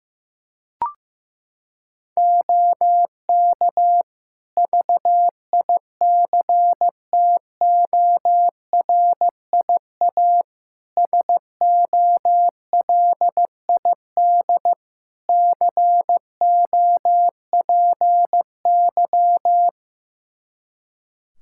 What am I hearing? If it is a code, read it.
Morse code, heard as a keyed tone, about 15 wpm, 700 Hz. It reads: OK VICTORIA SOLID COPY
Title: Practice QSO - Exchange 3 - More Info - 15wpm